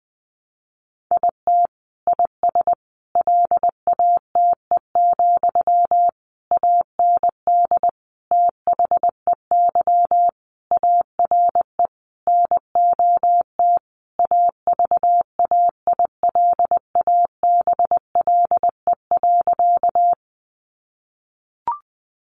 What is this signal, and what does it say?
Morse code, heard as a keyed tone, about 20 wpm, 700 Hz. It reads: IT IS LATE, AND THEY ARE NOT AVAILABLE.
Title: Sentences from Top 600 Words 20wpm